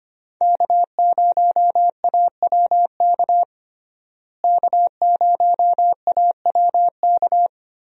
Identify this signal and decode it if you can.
Morse code, heard as a keyed tone, about 25 wpm, 700 Hz. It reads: K0AWK K0AWK
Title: USA Call Signs - 5 Characters - Repeated 3x - 25wpm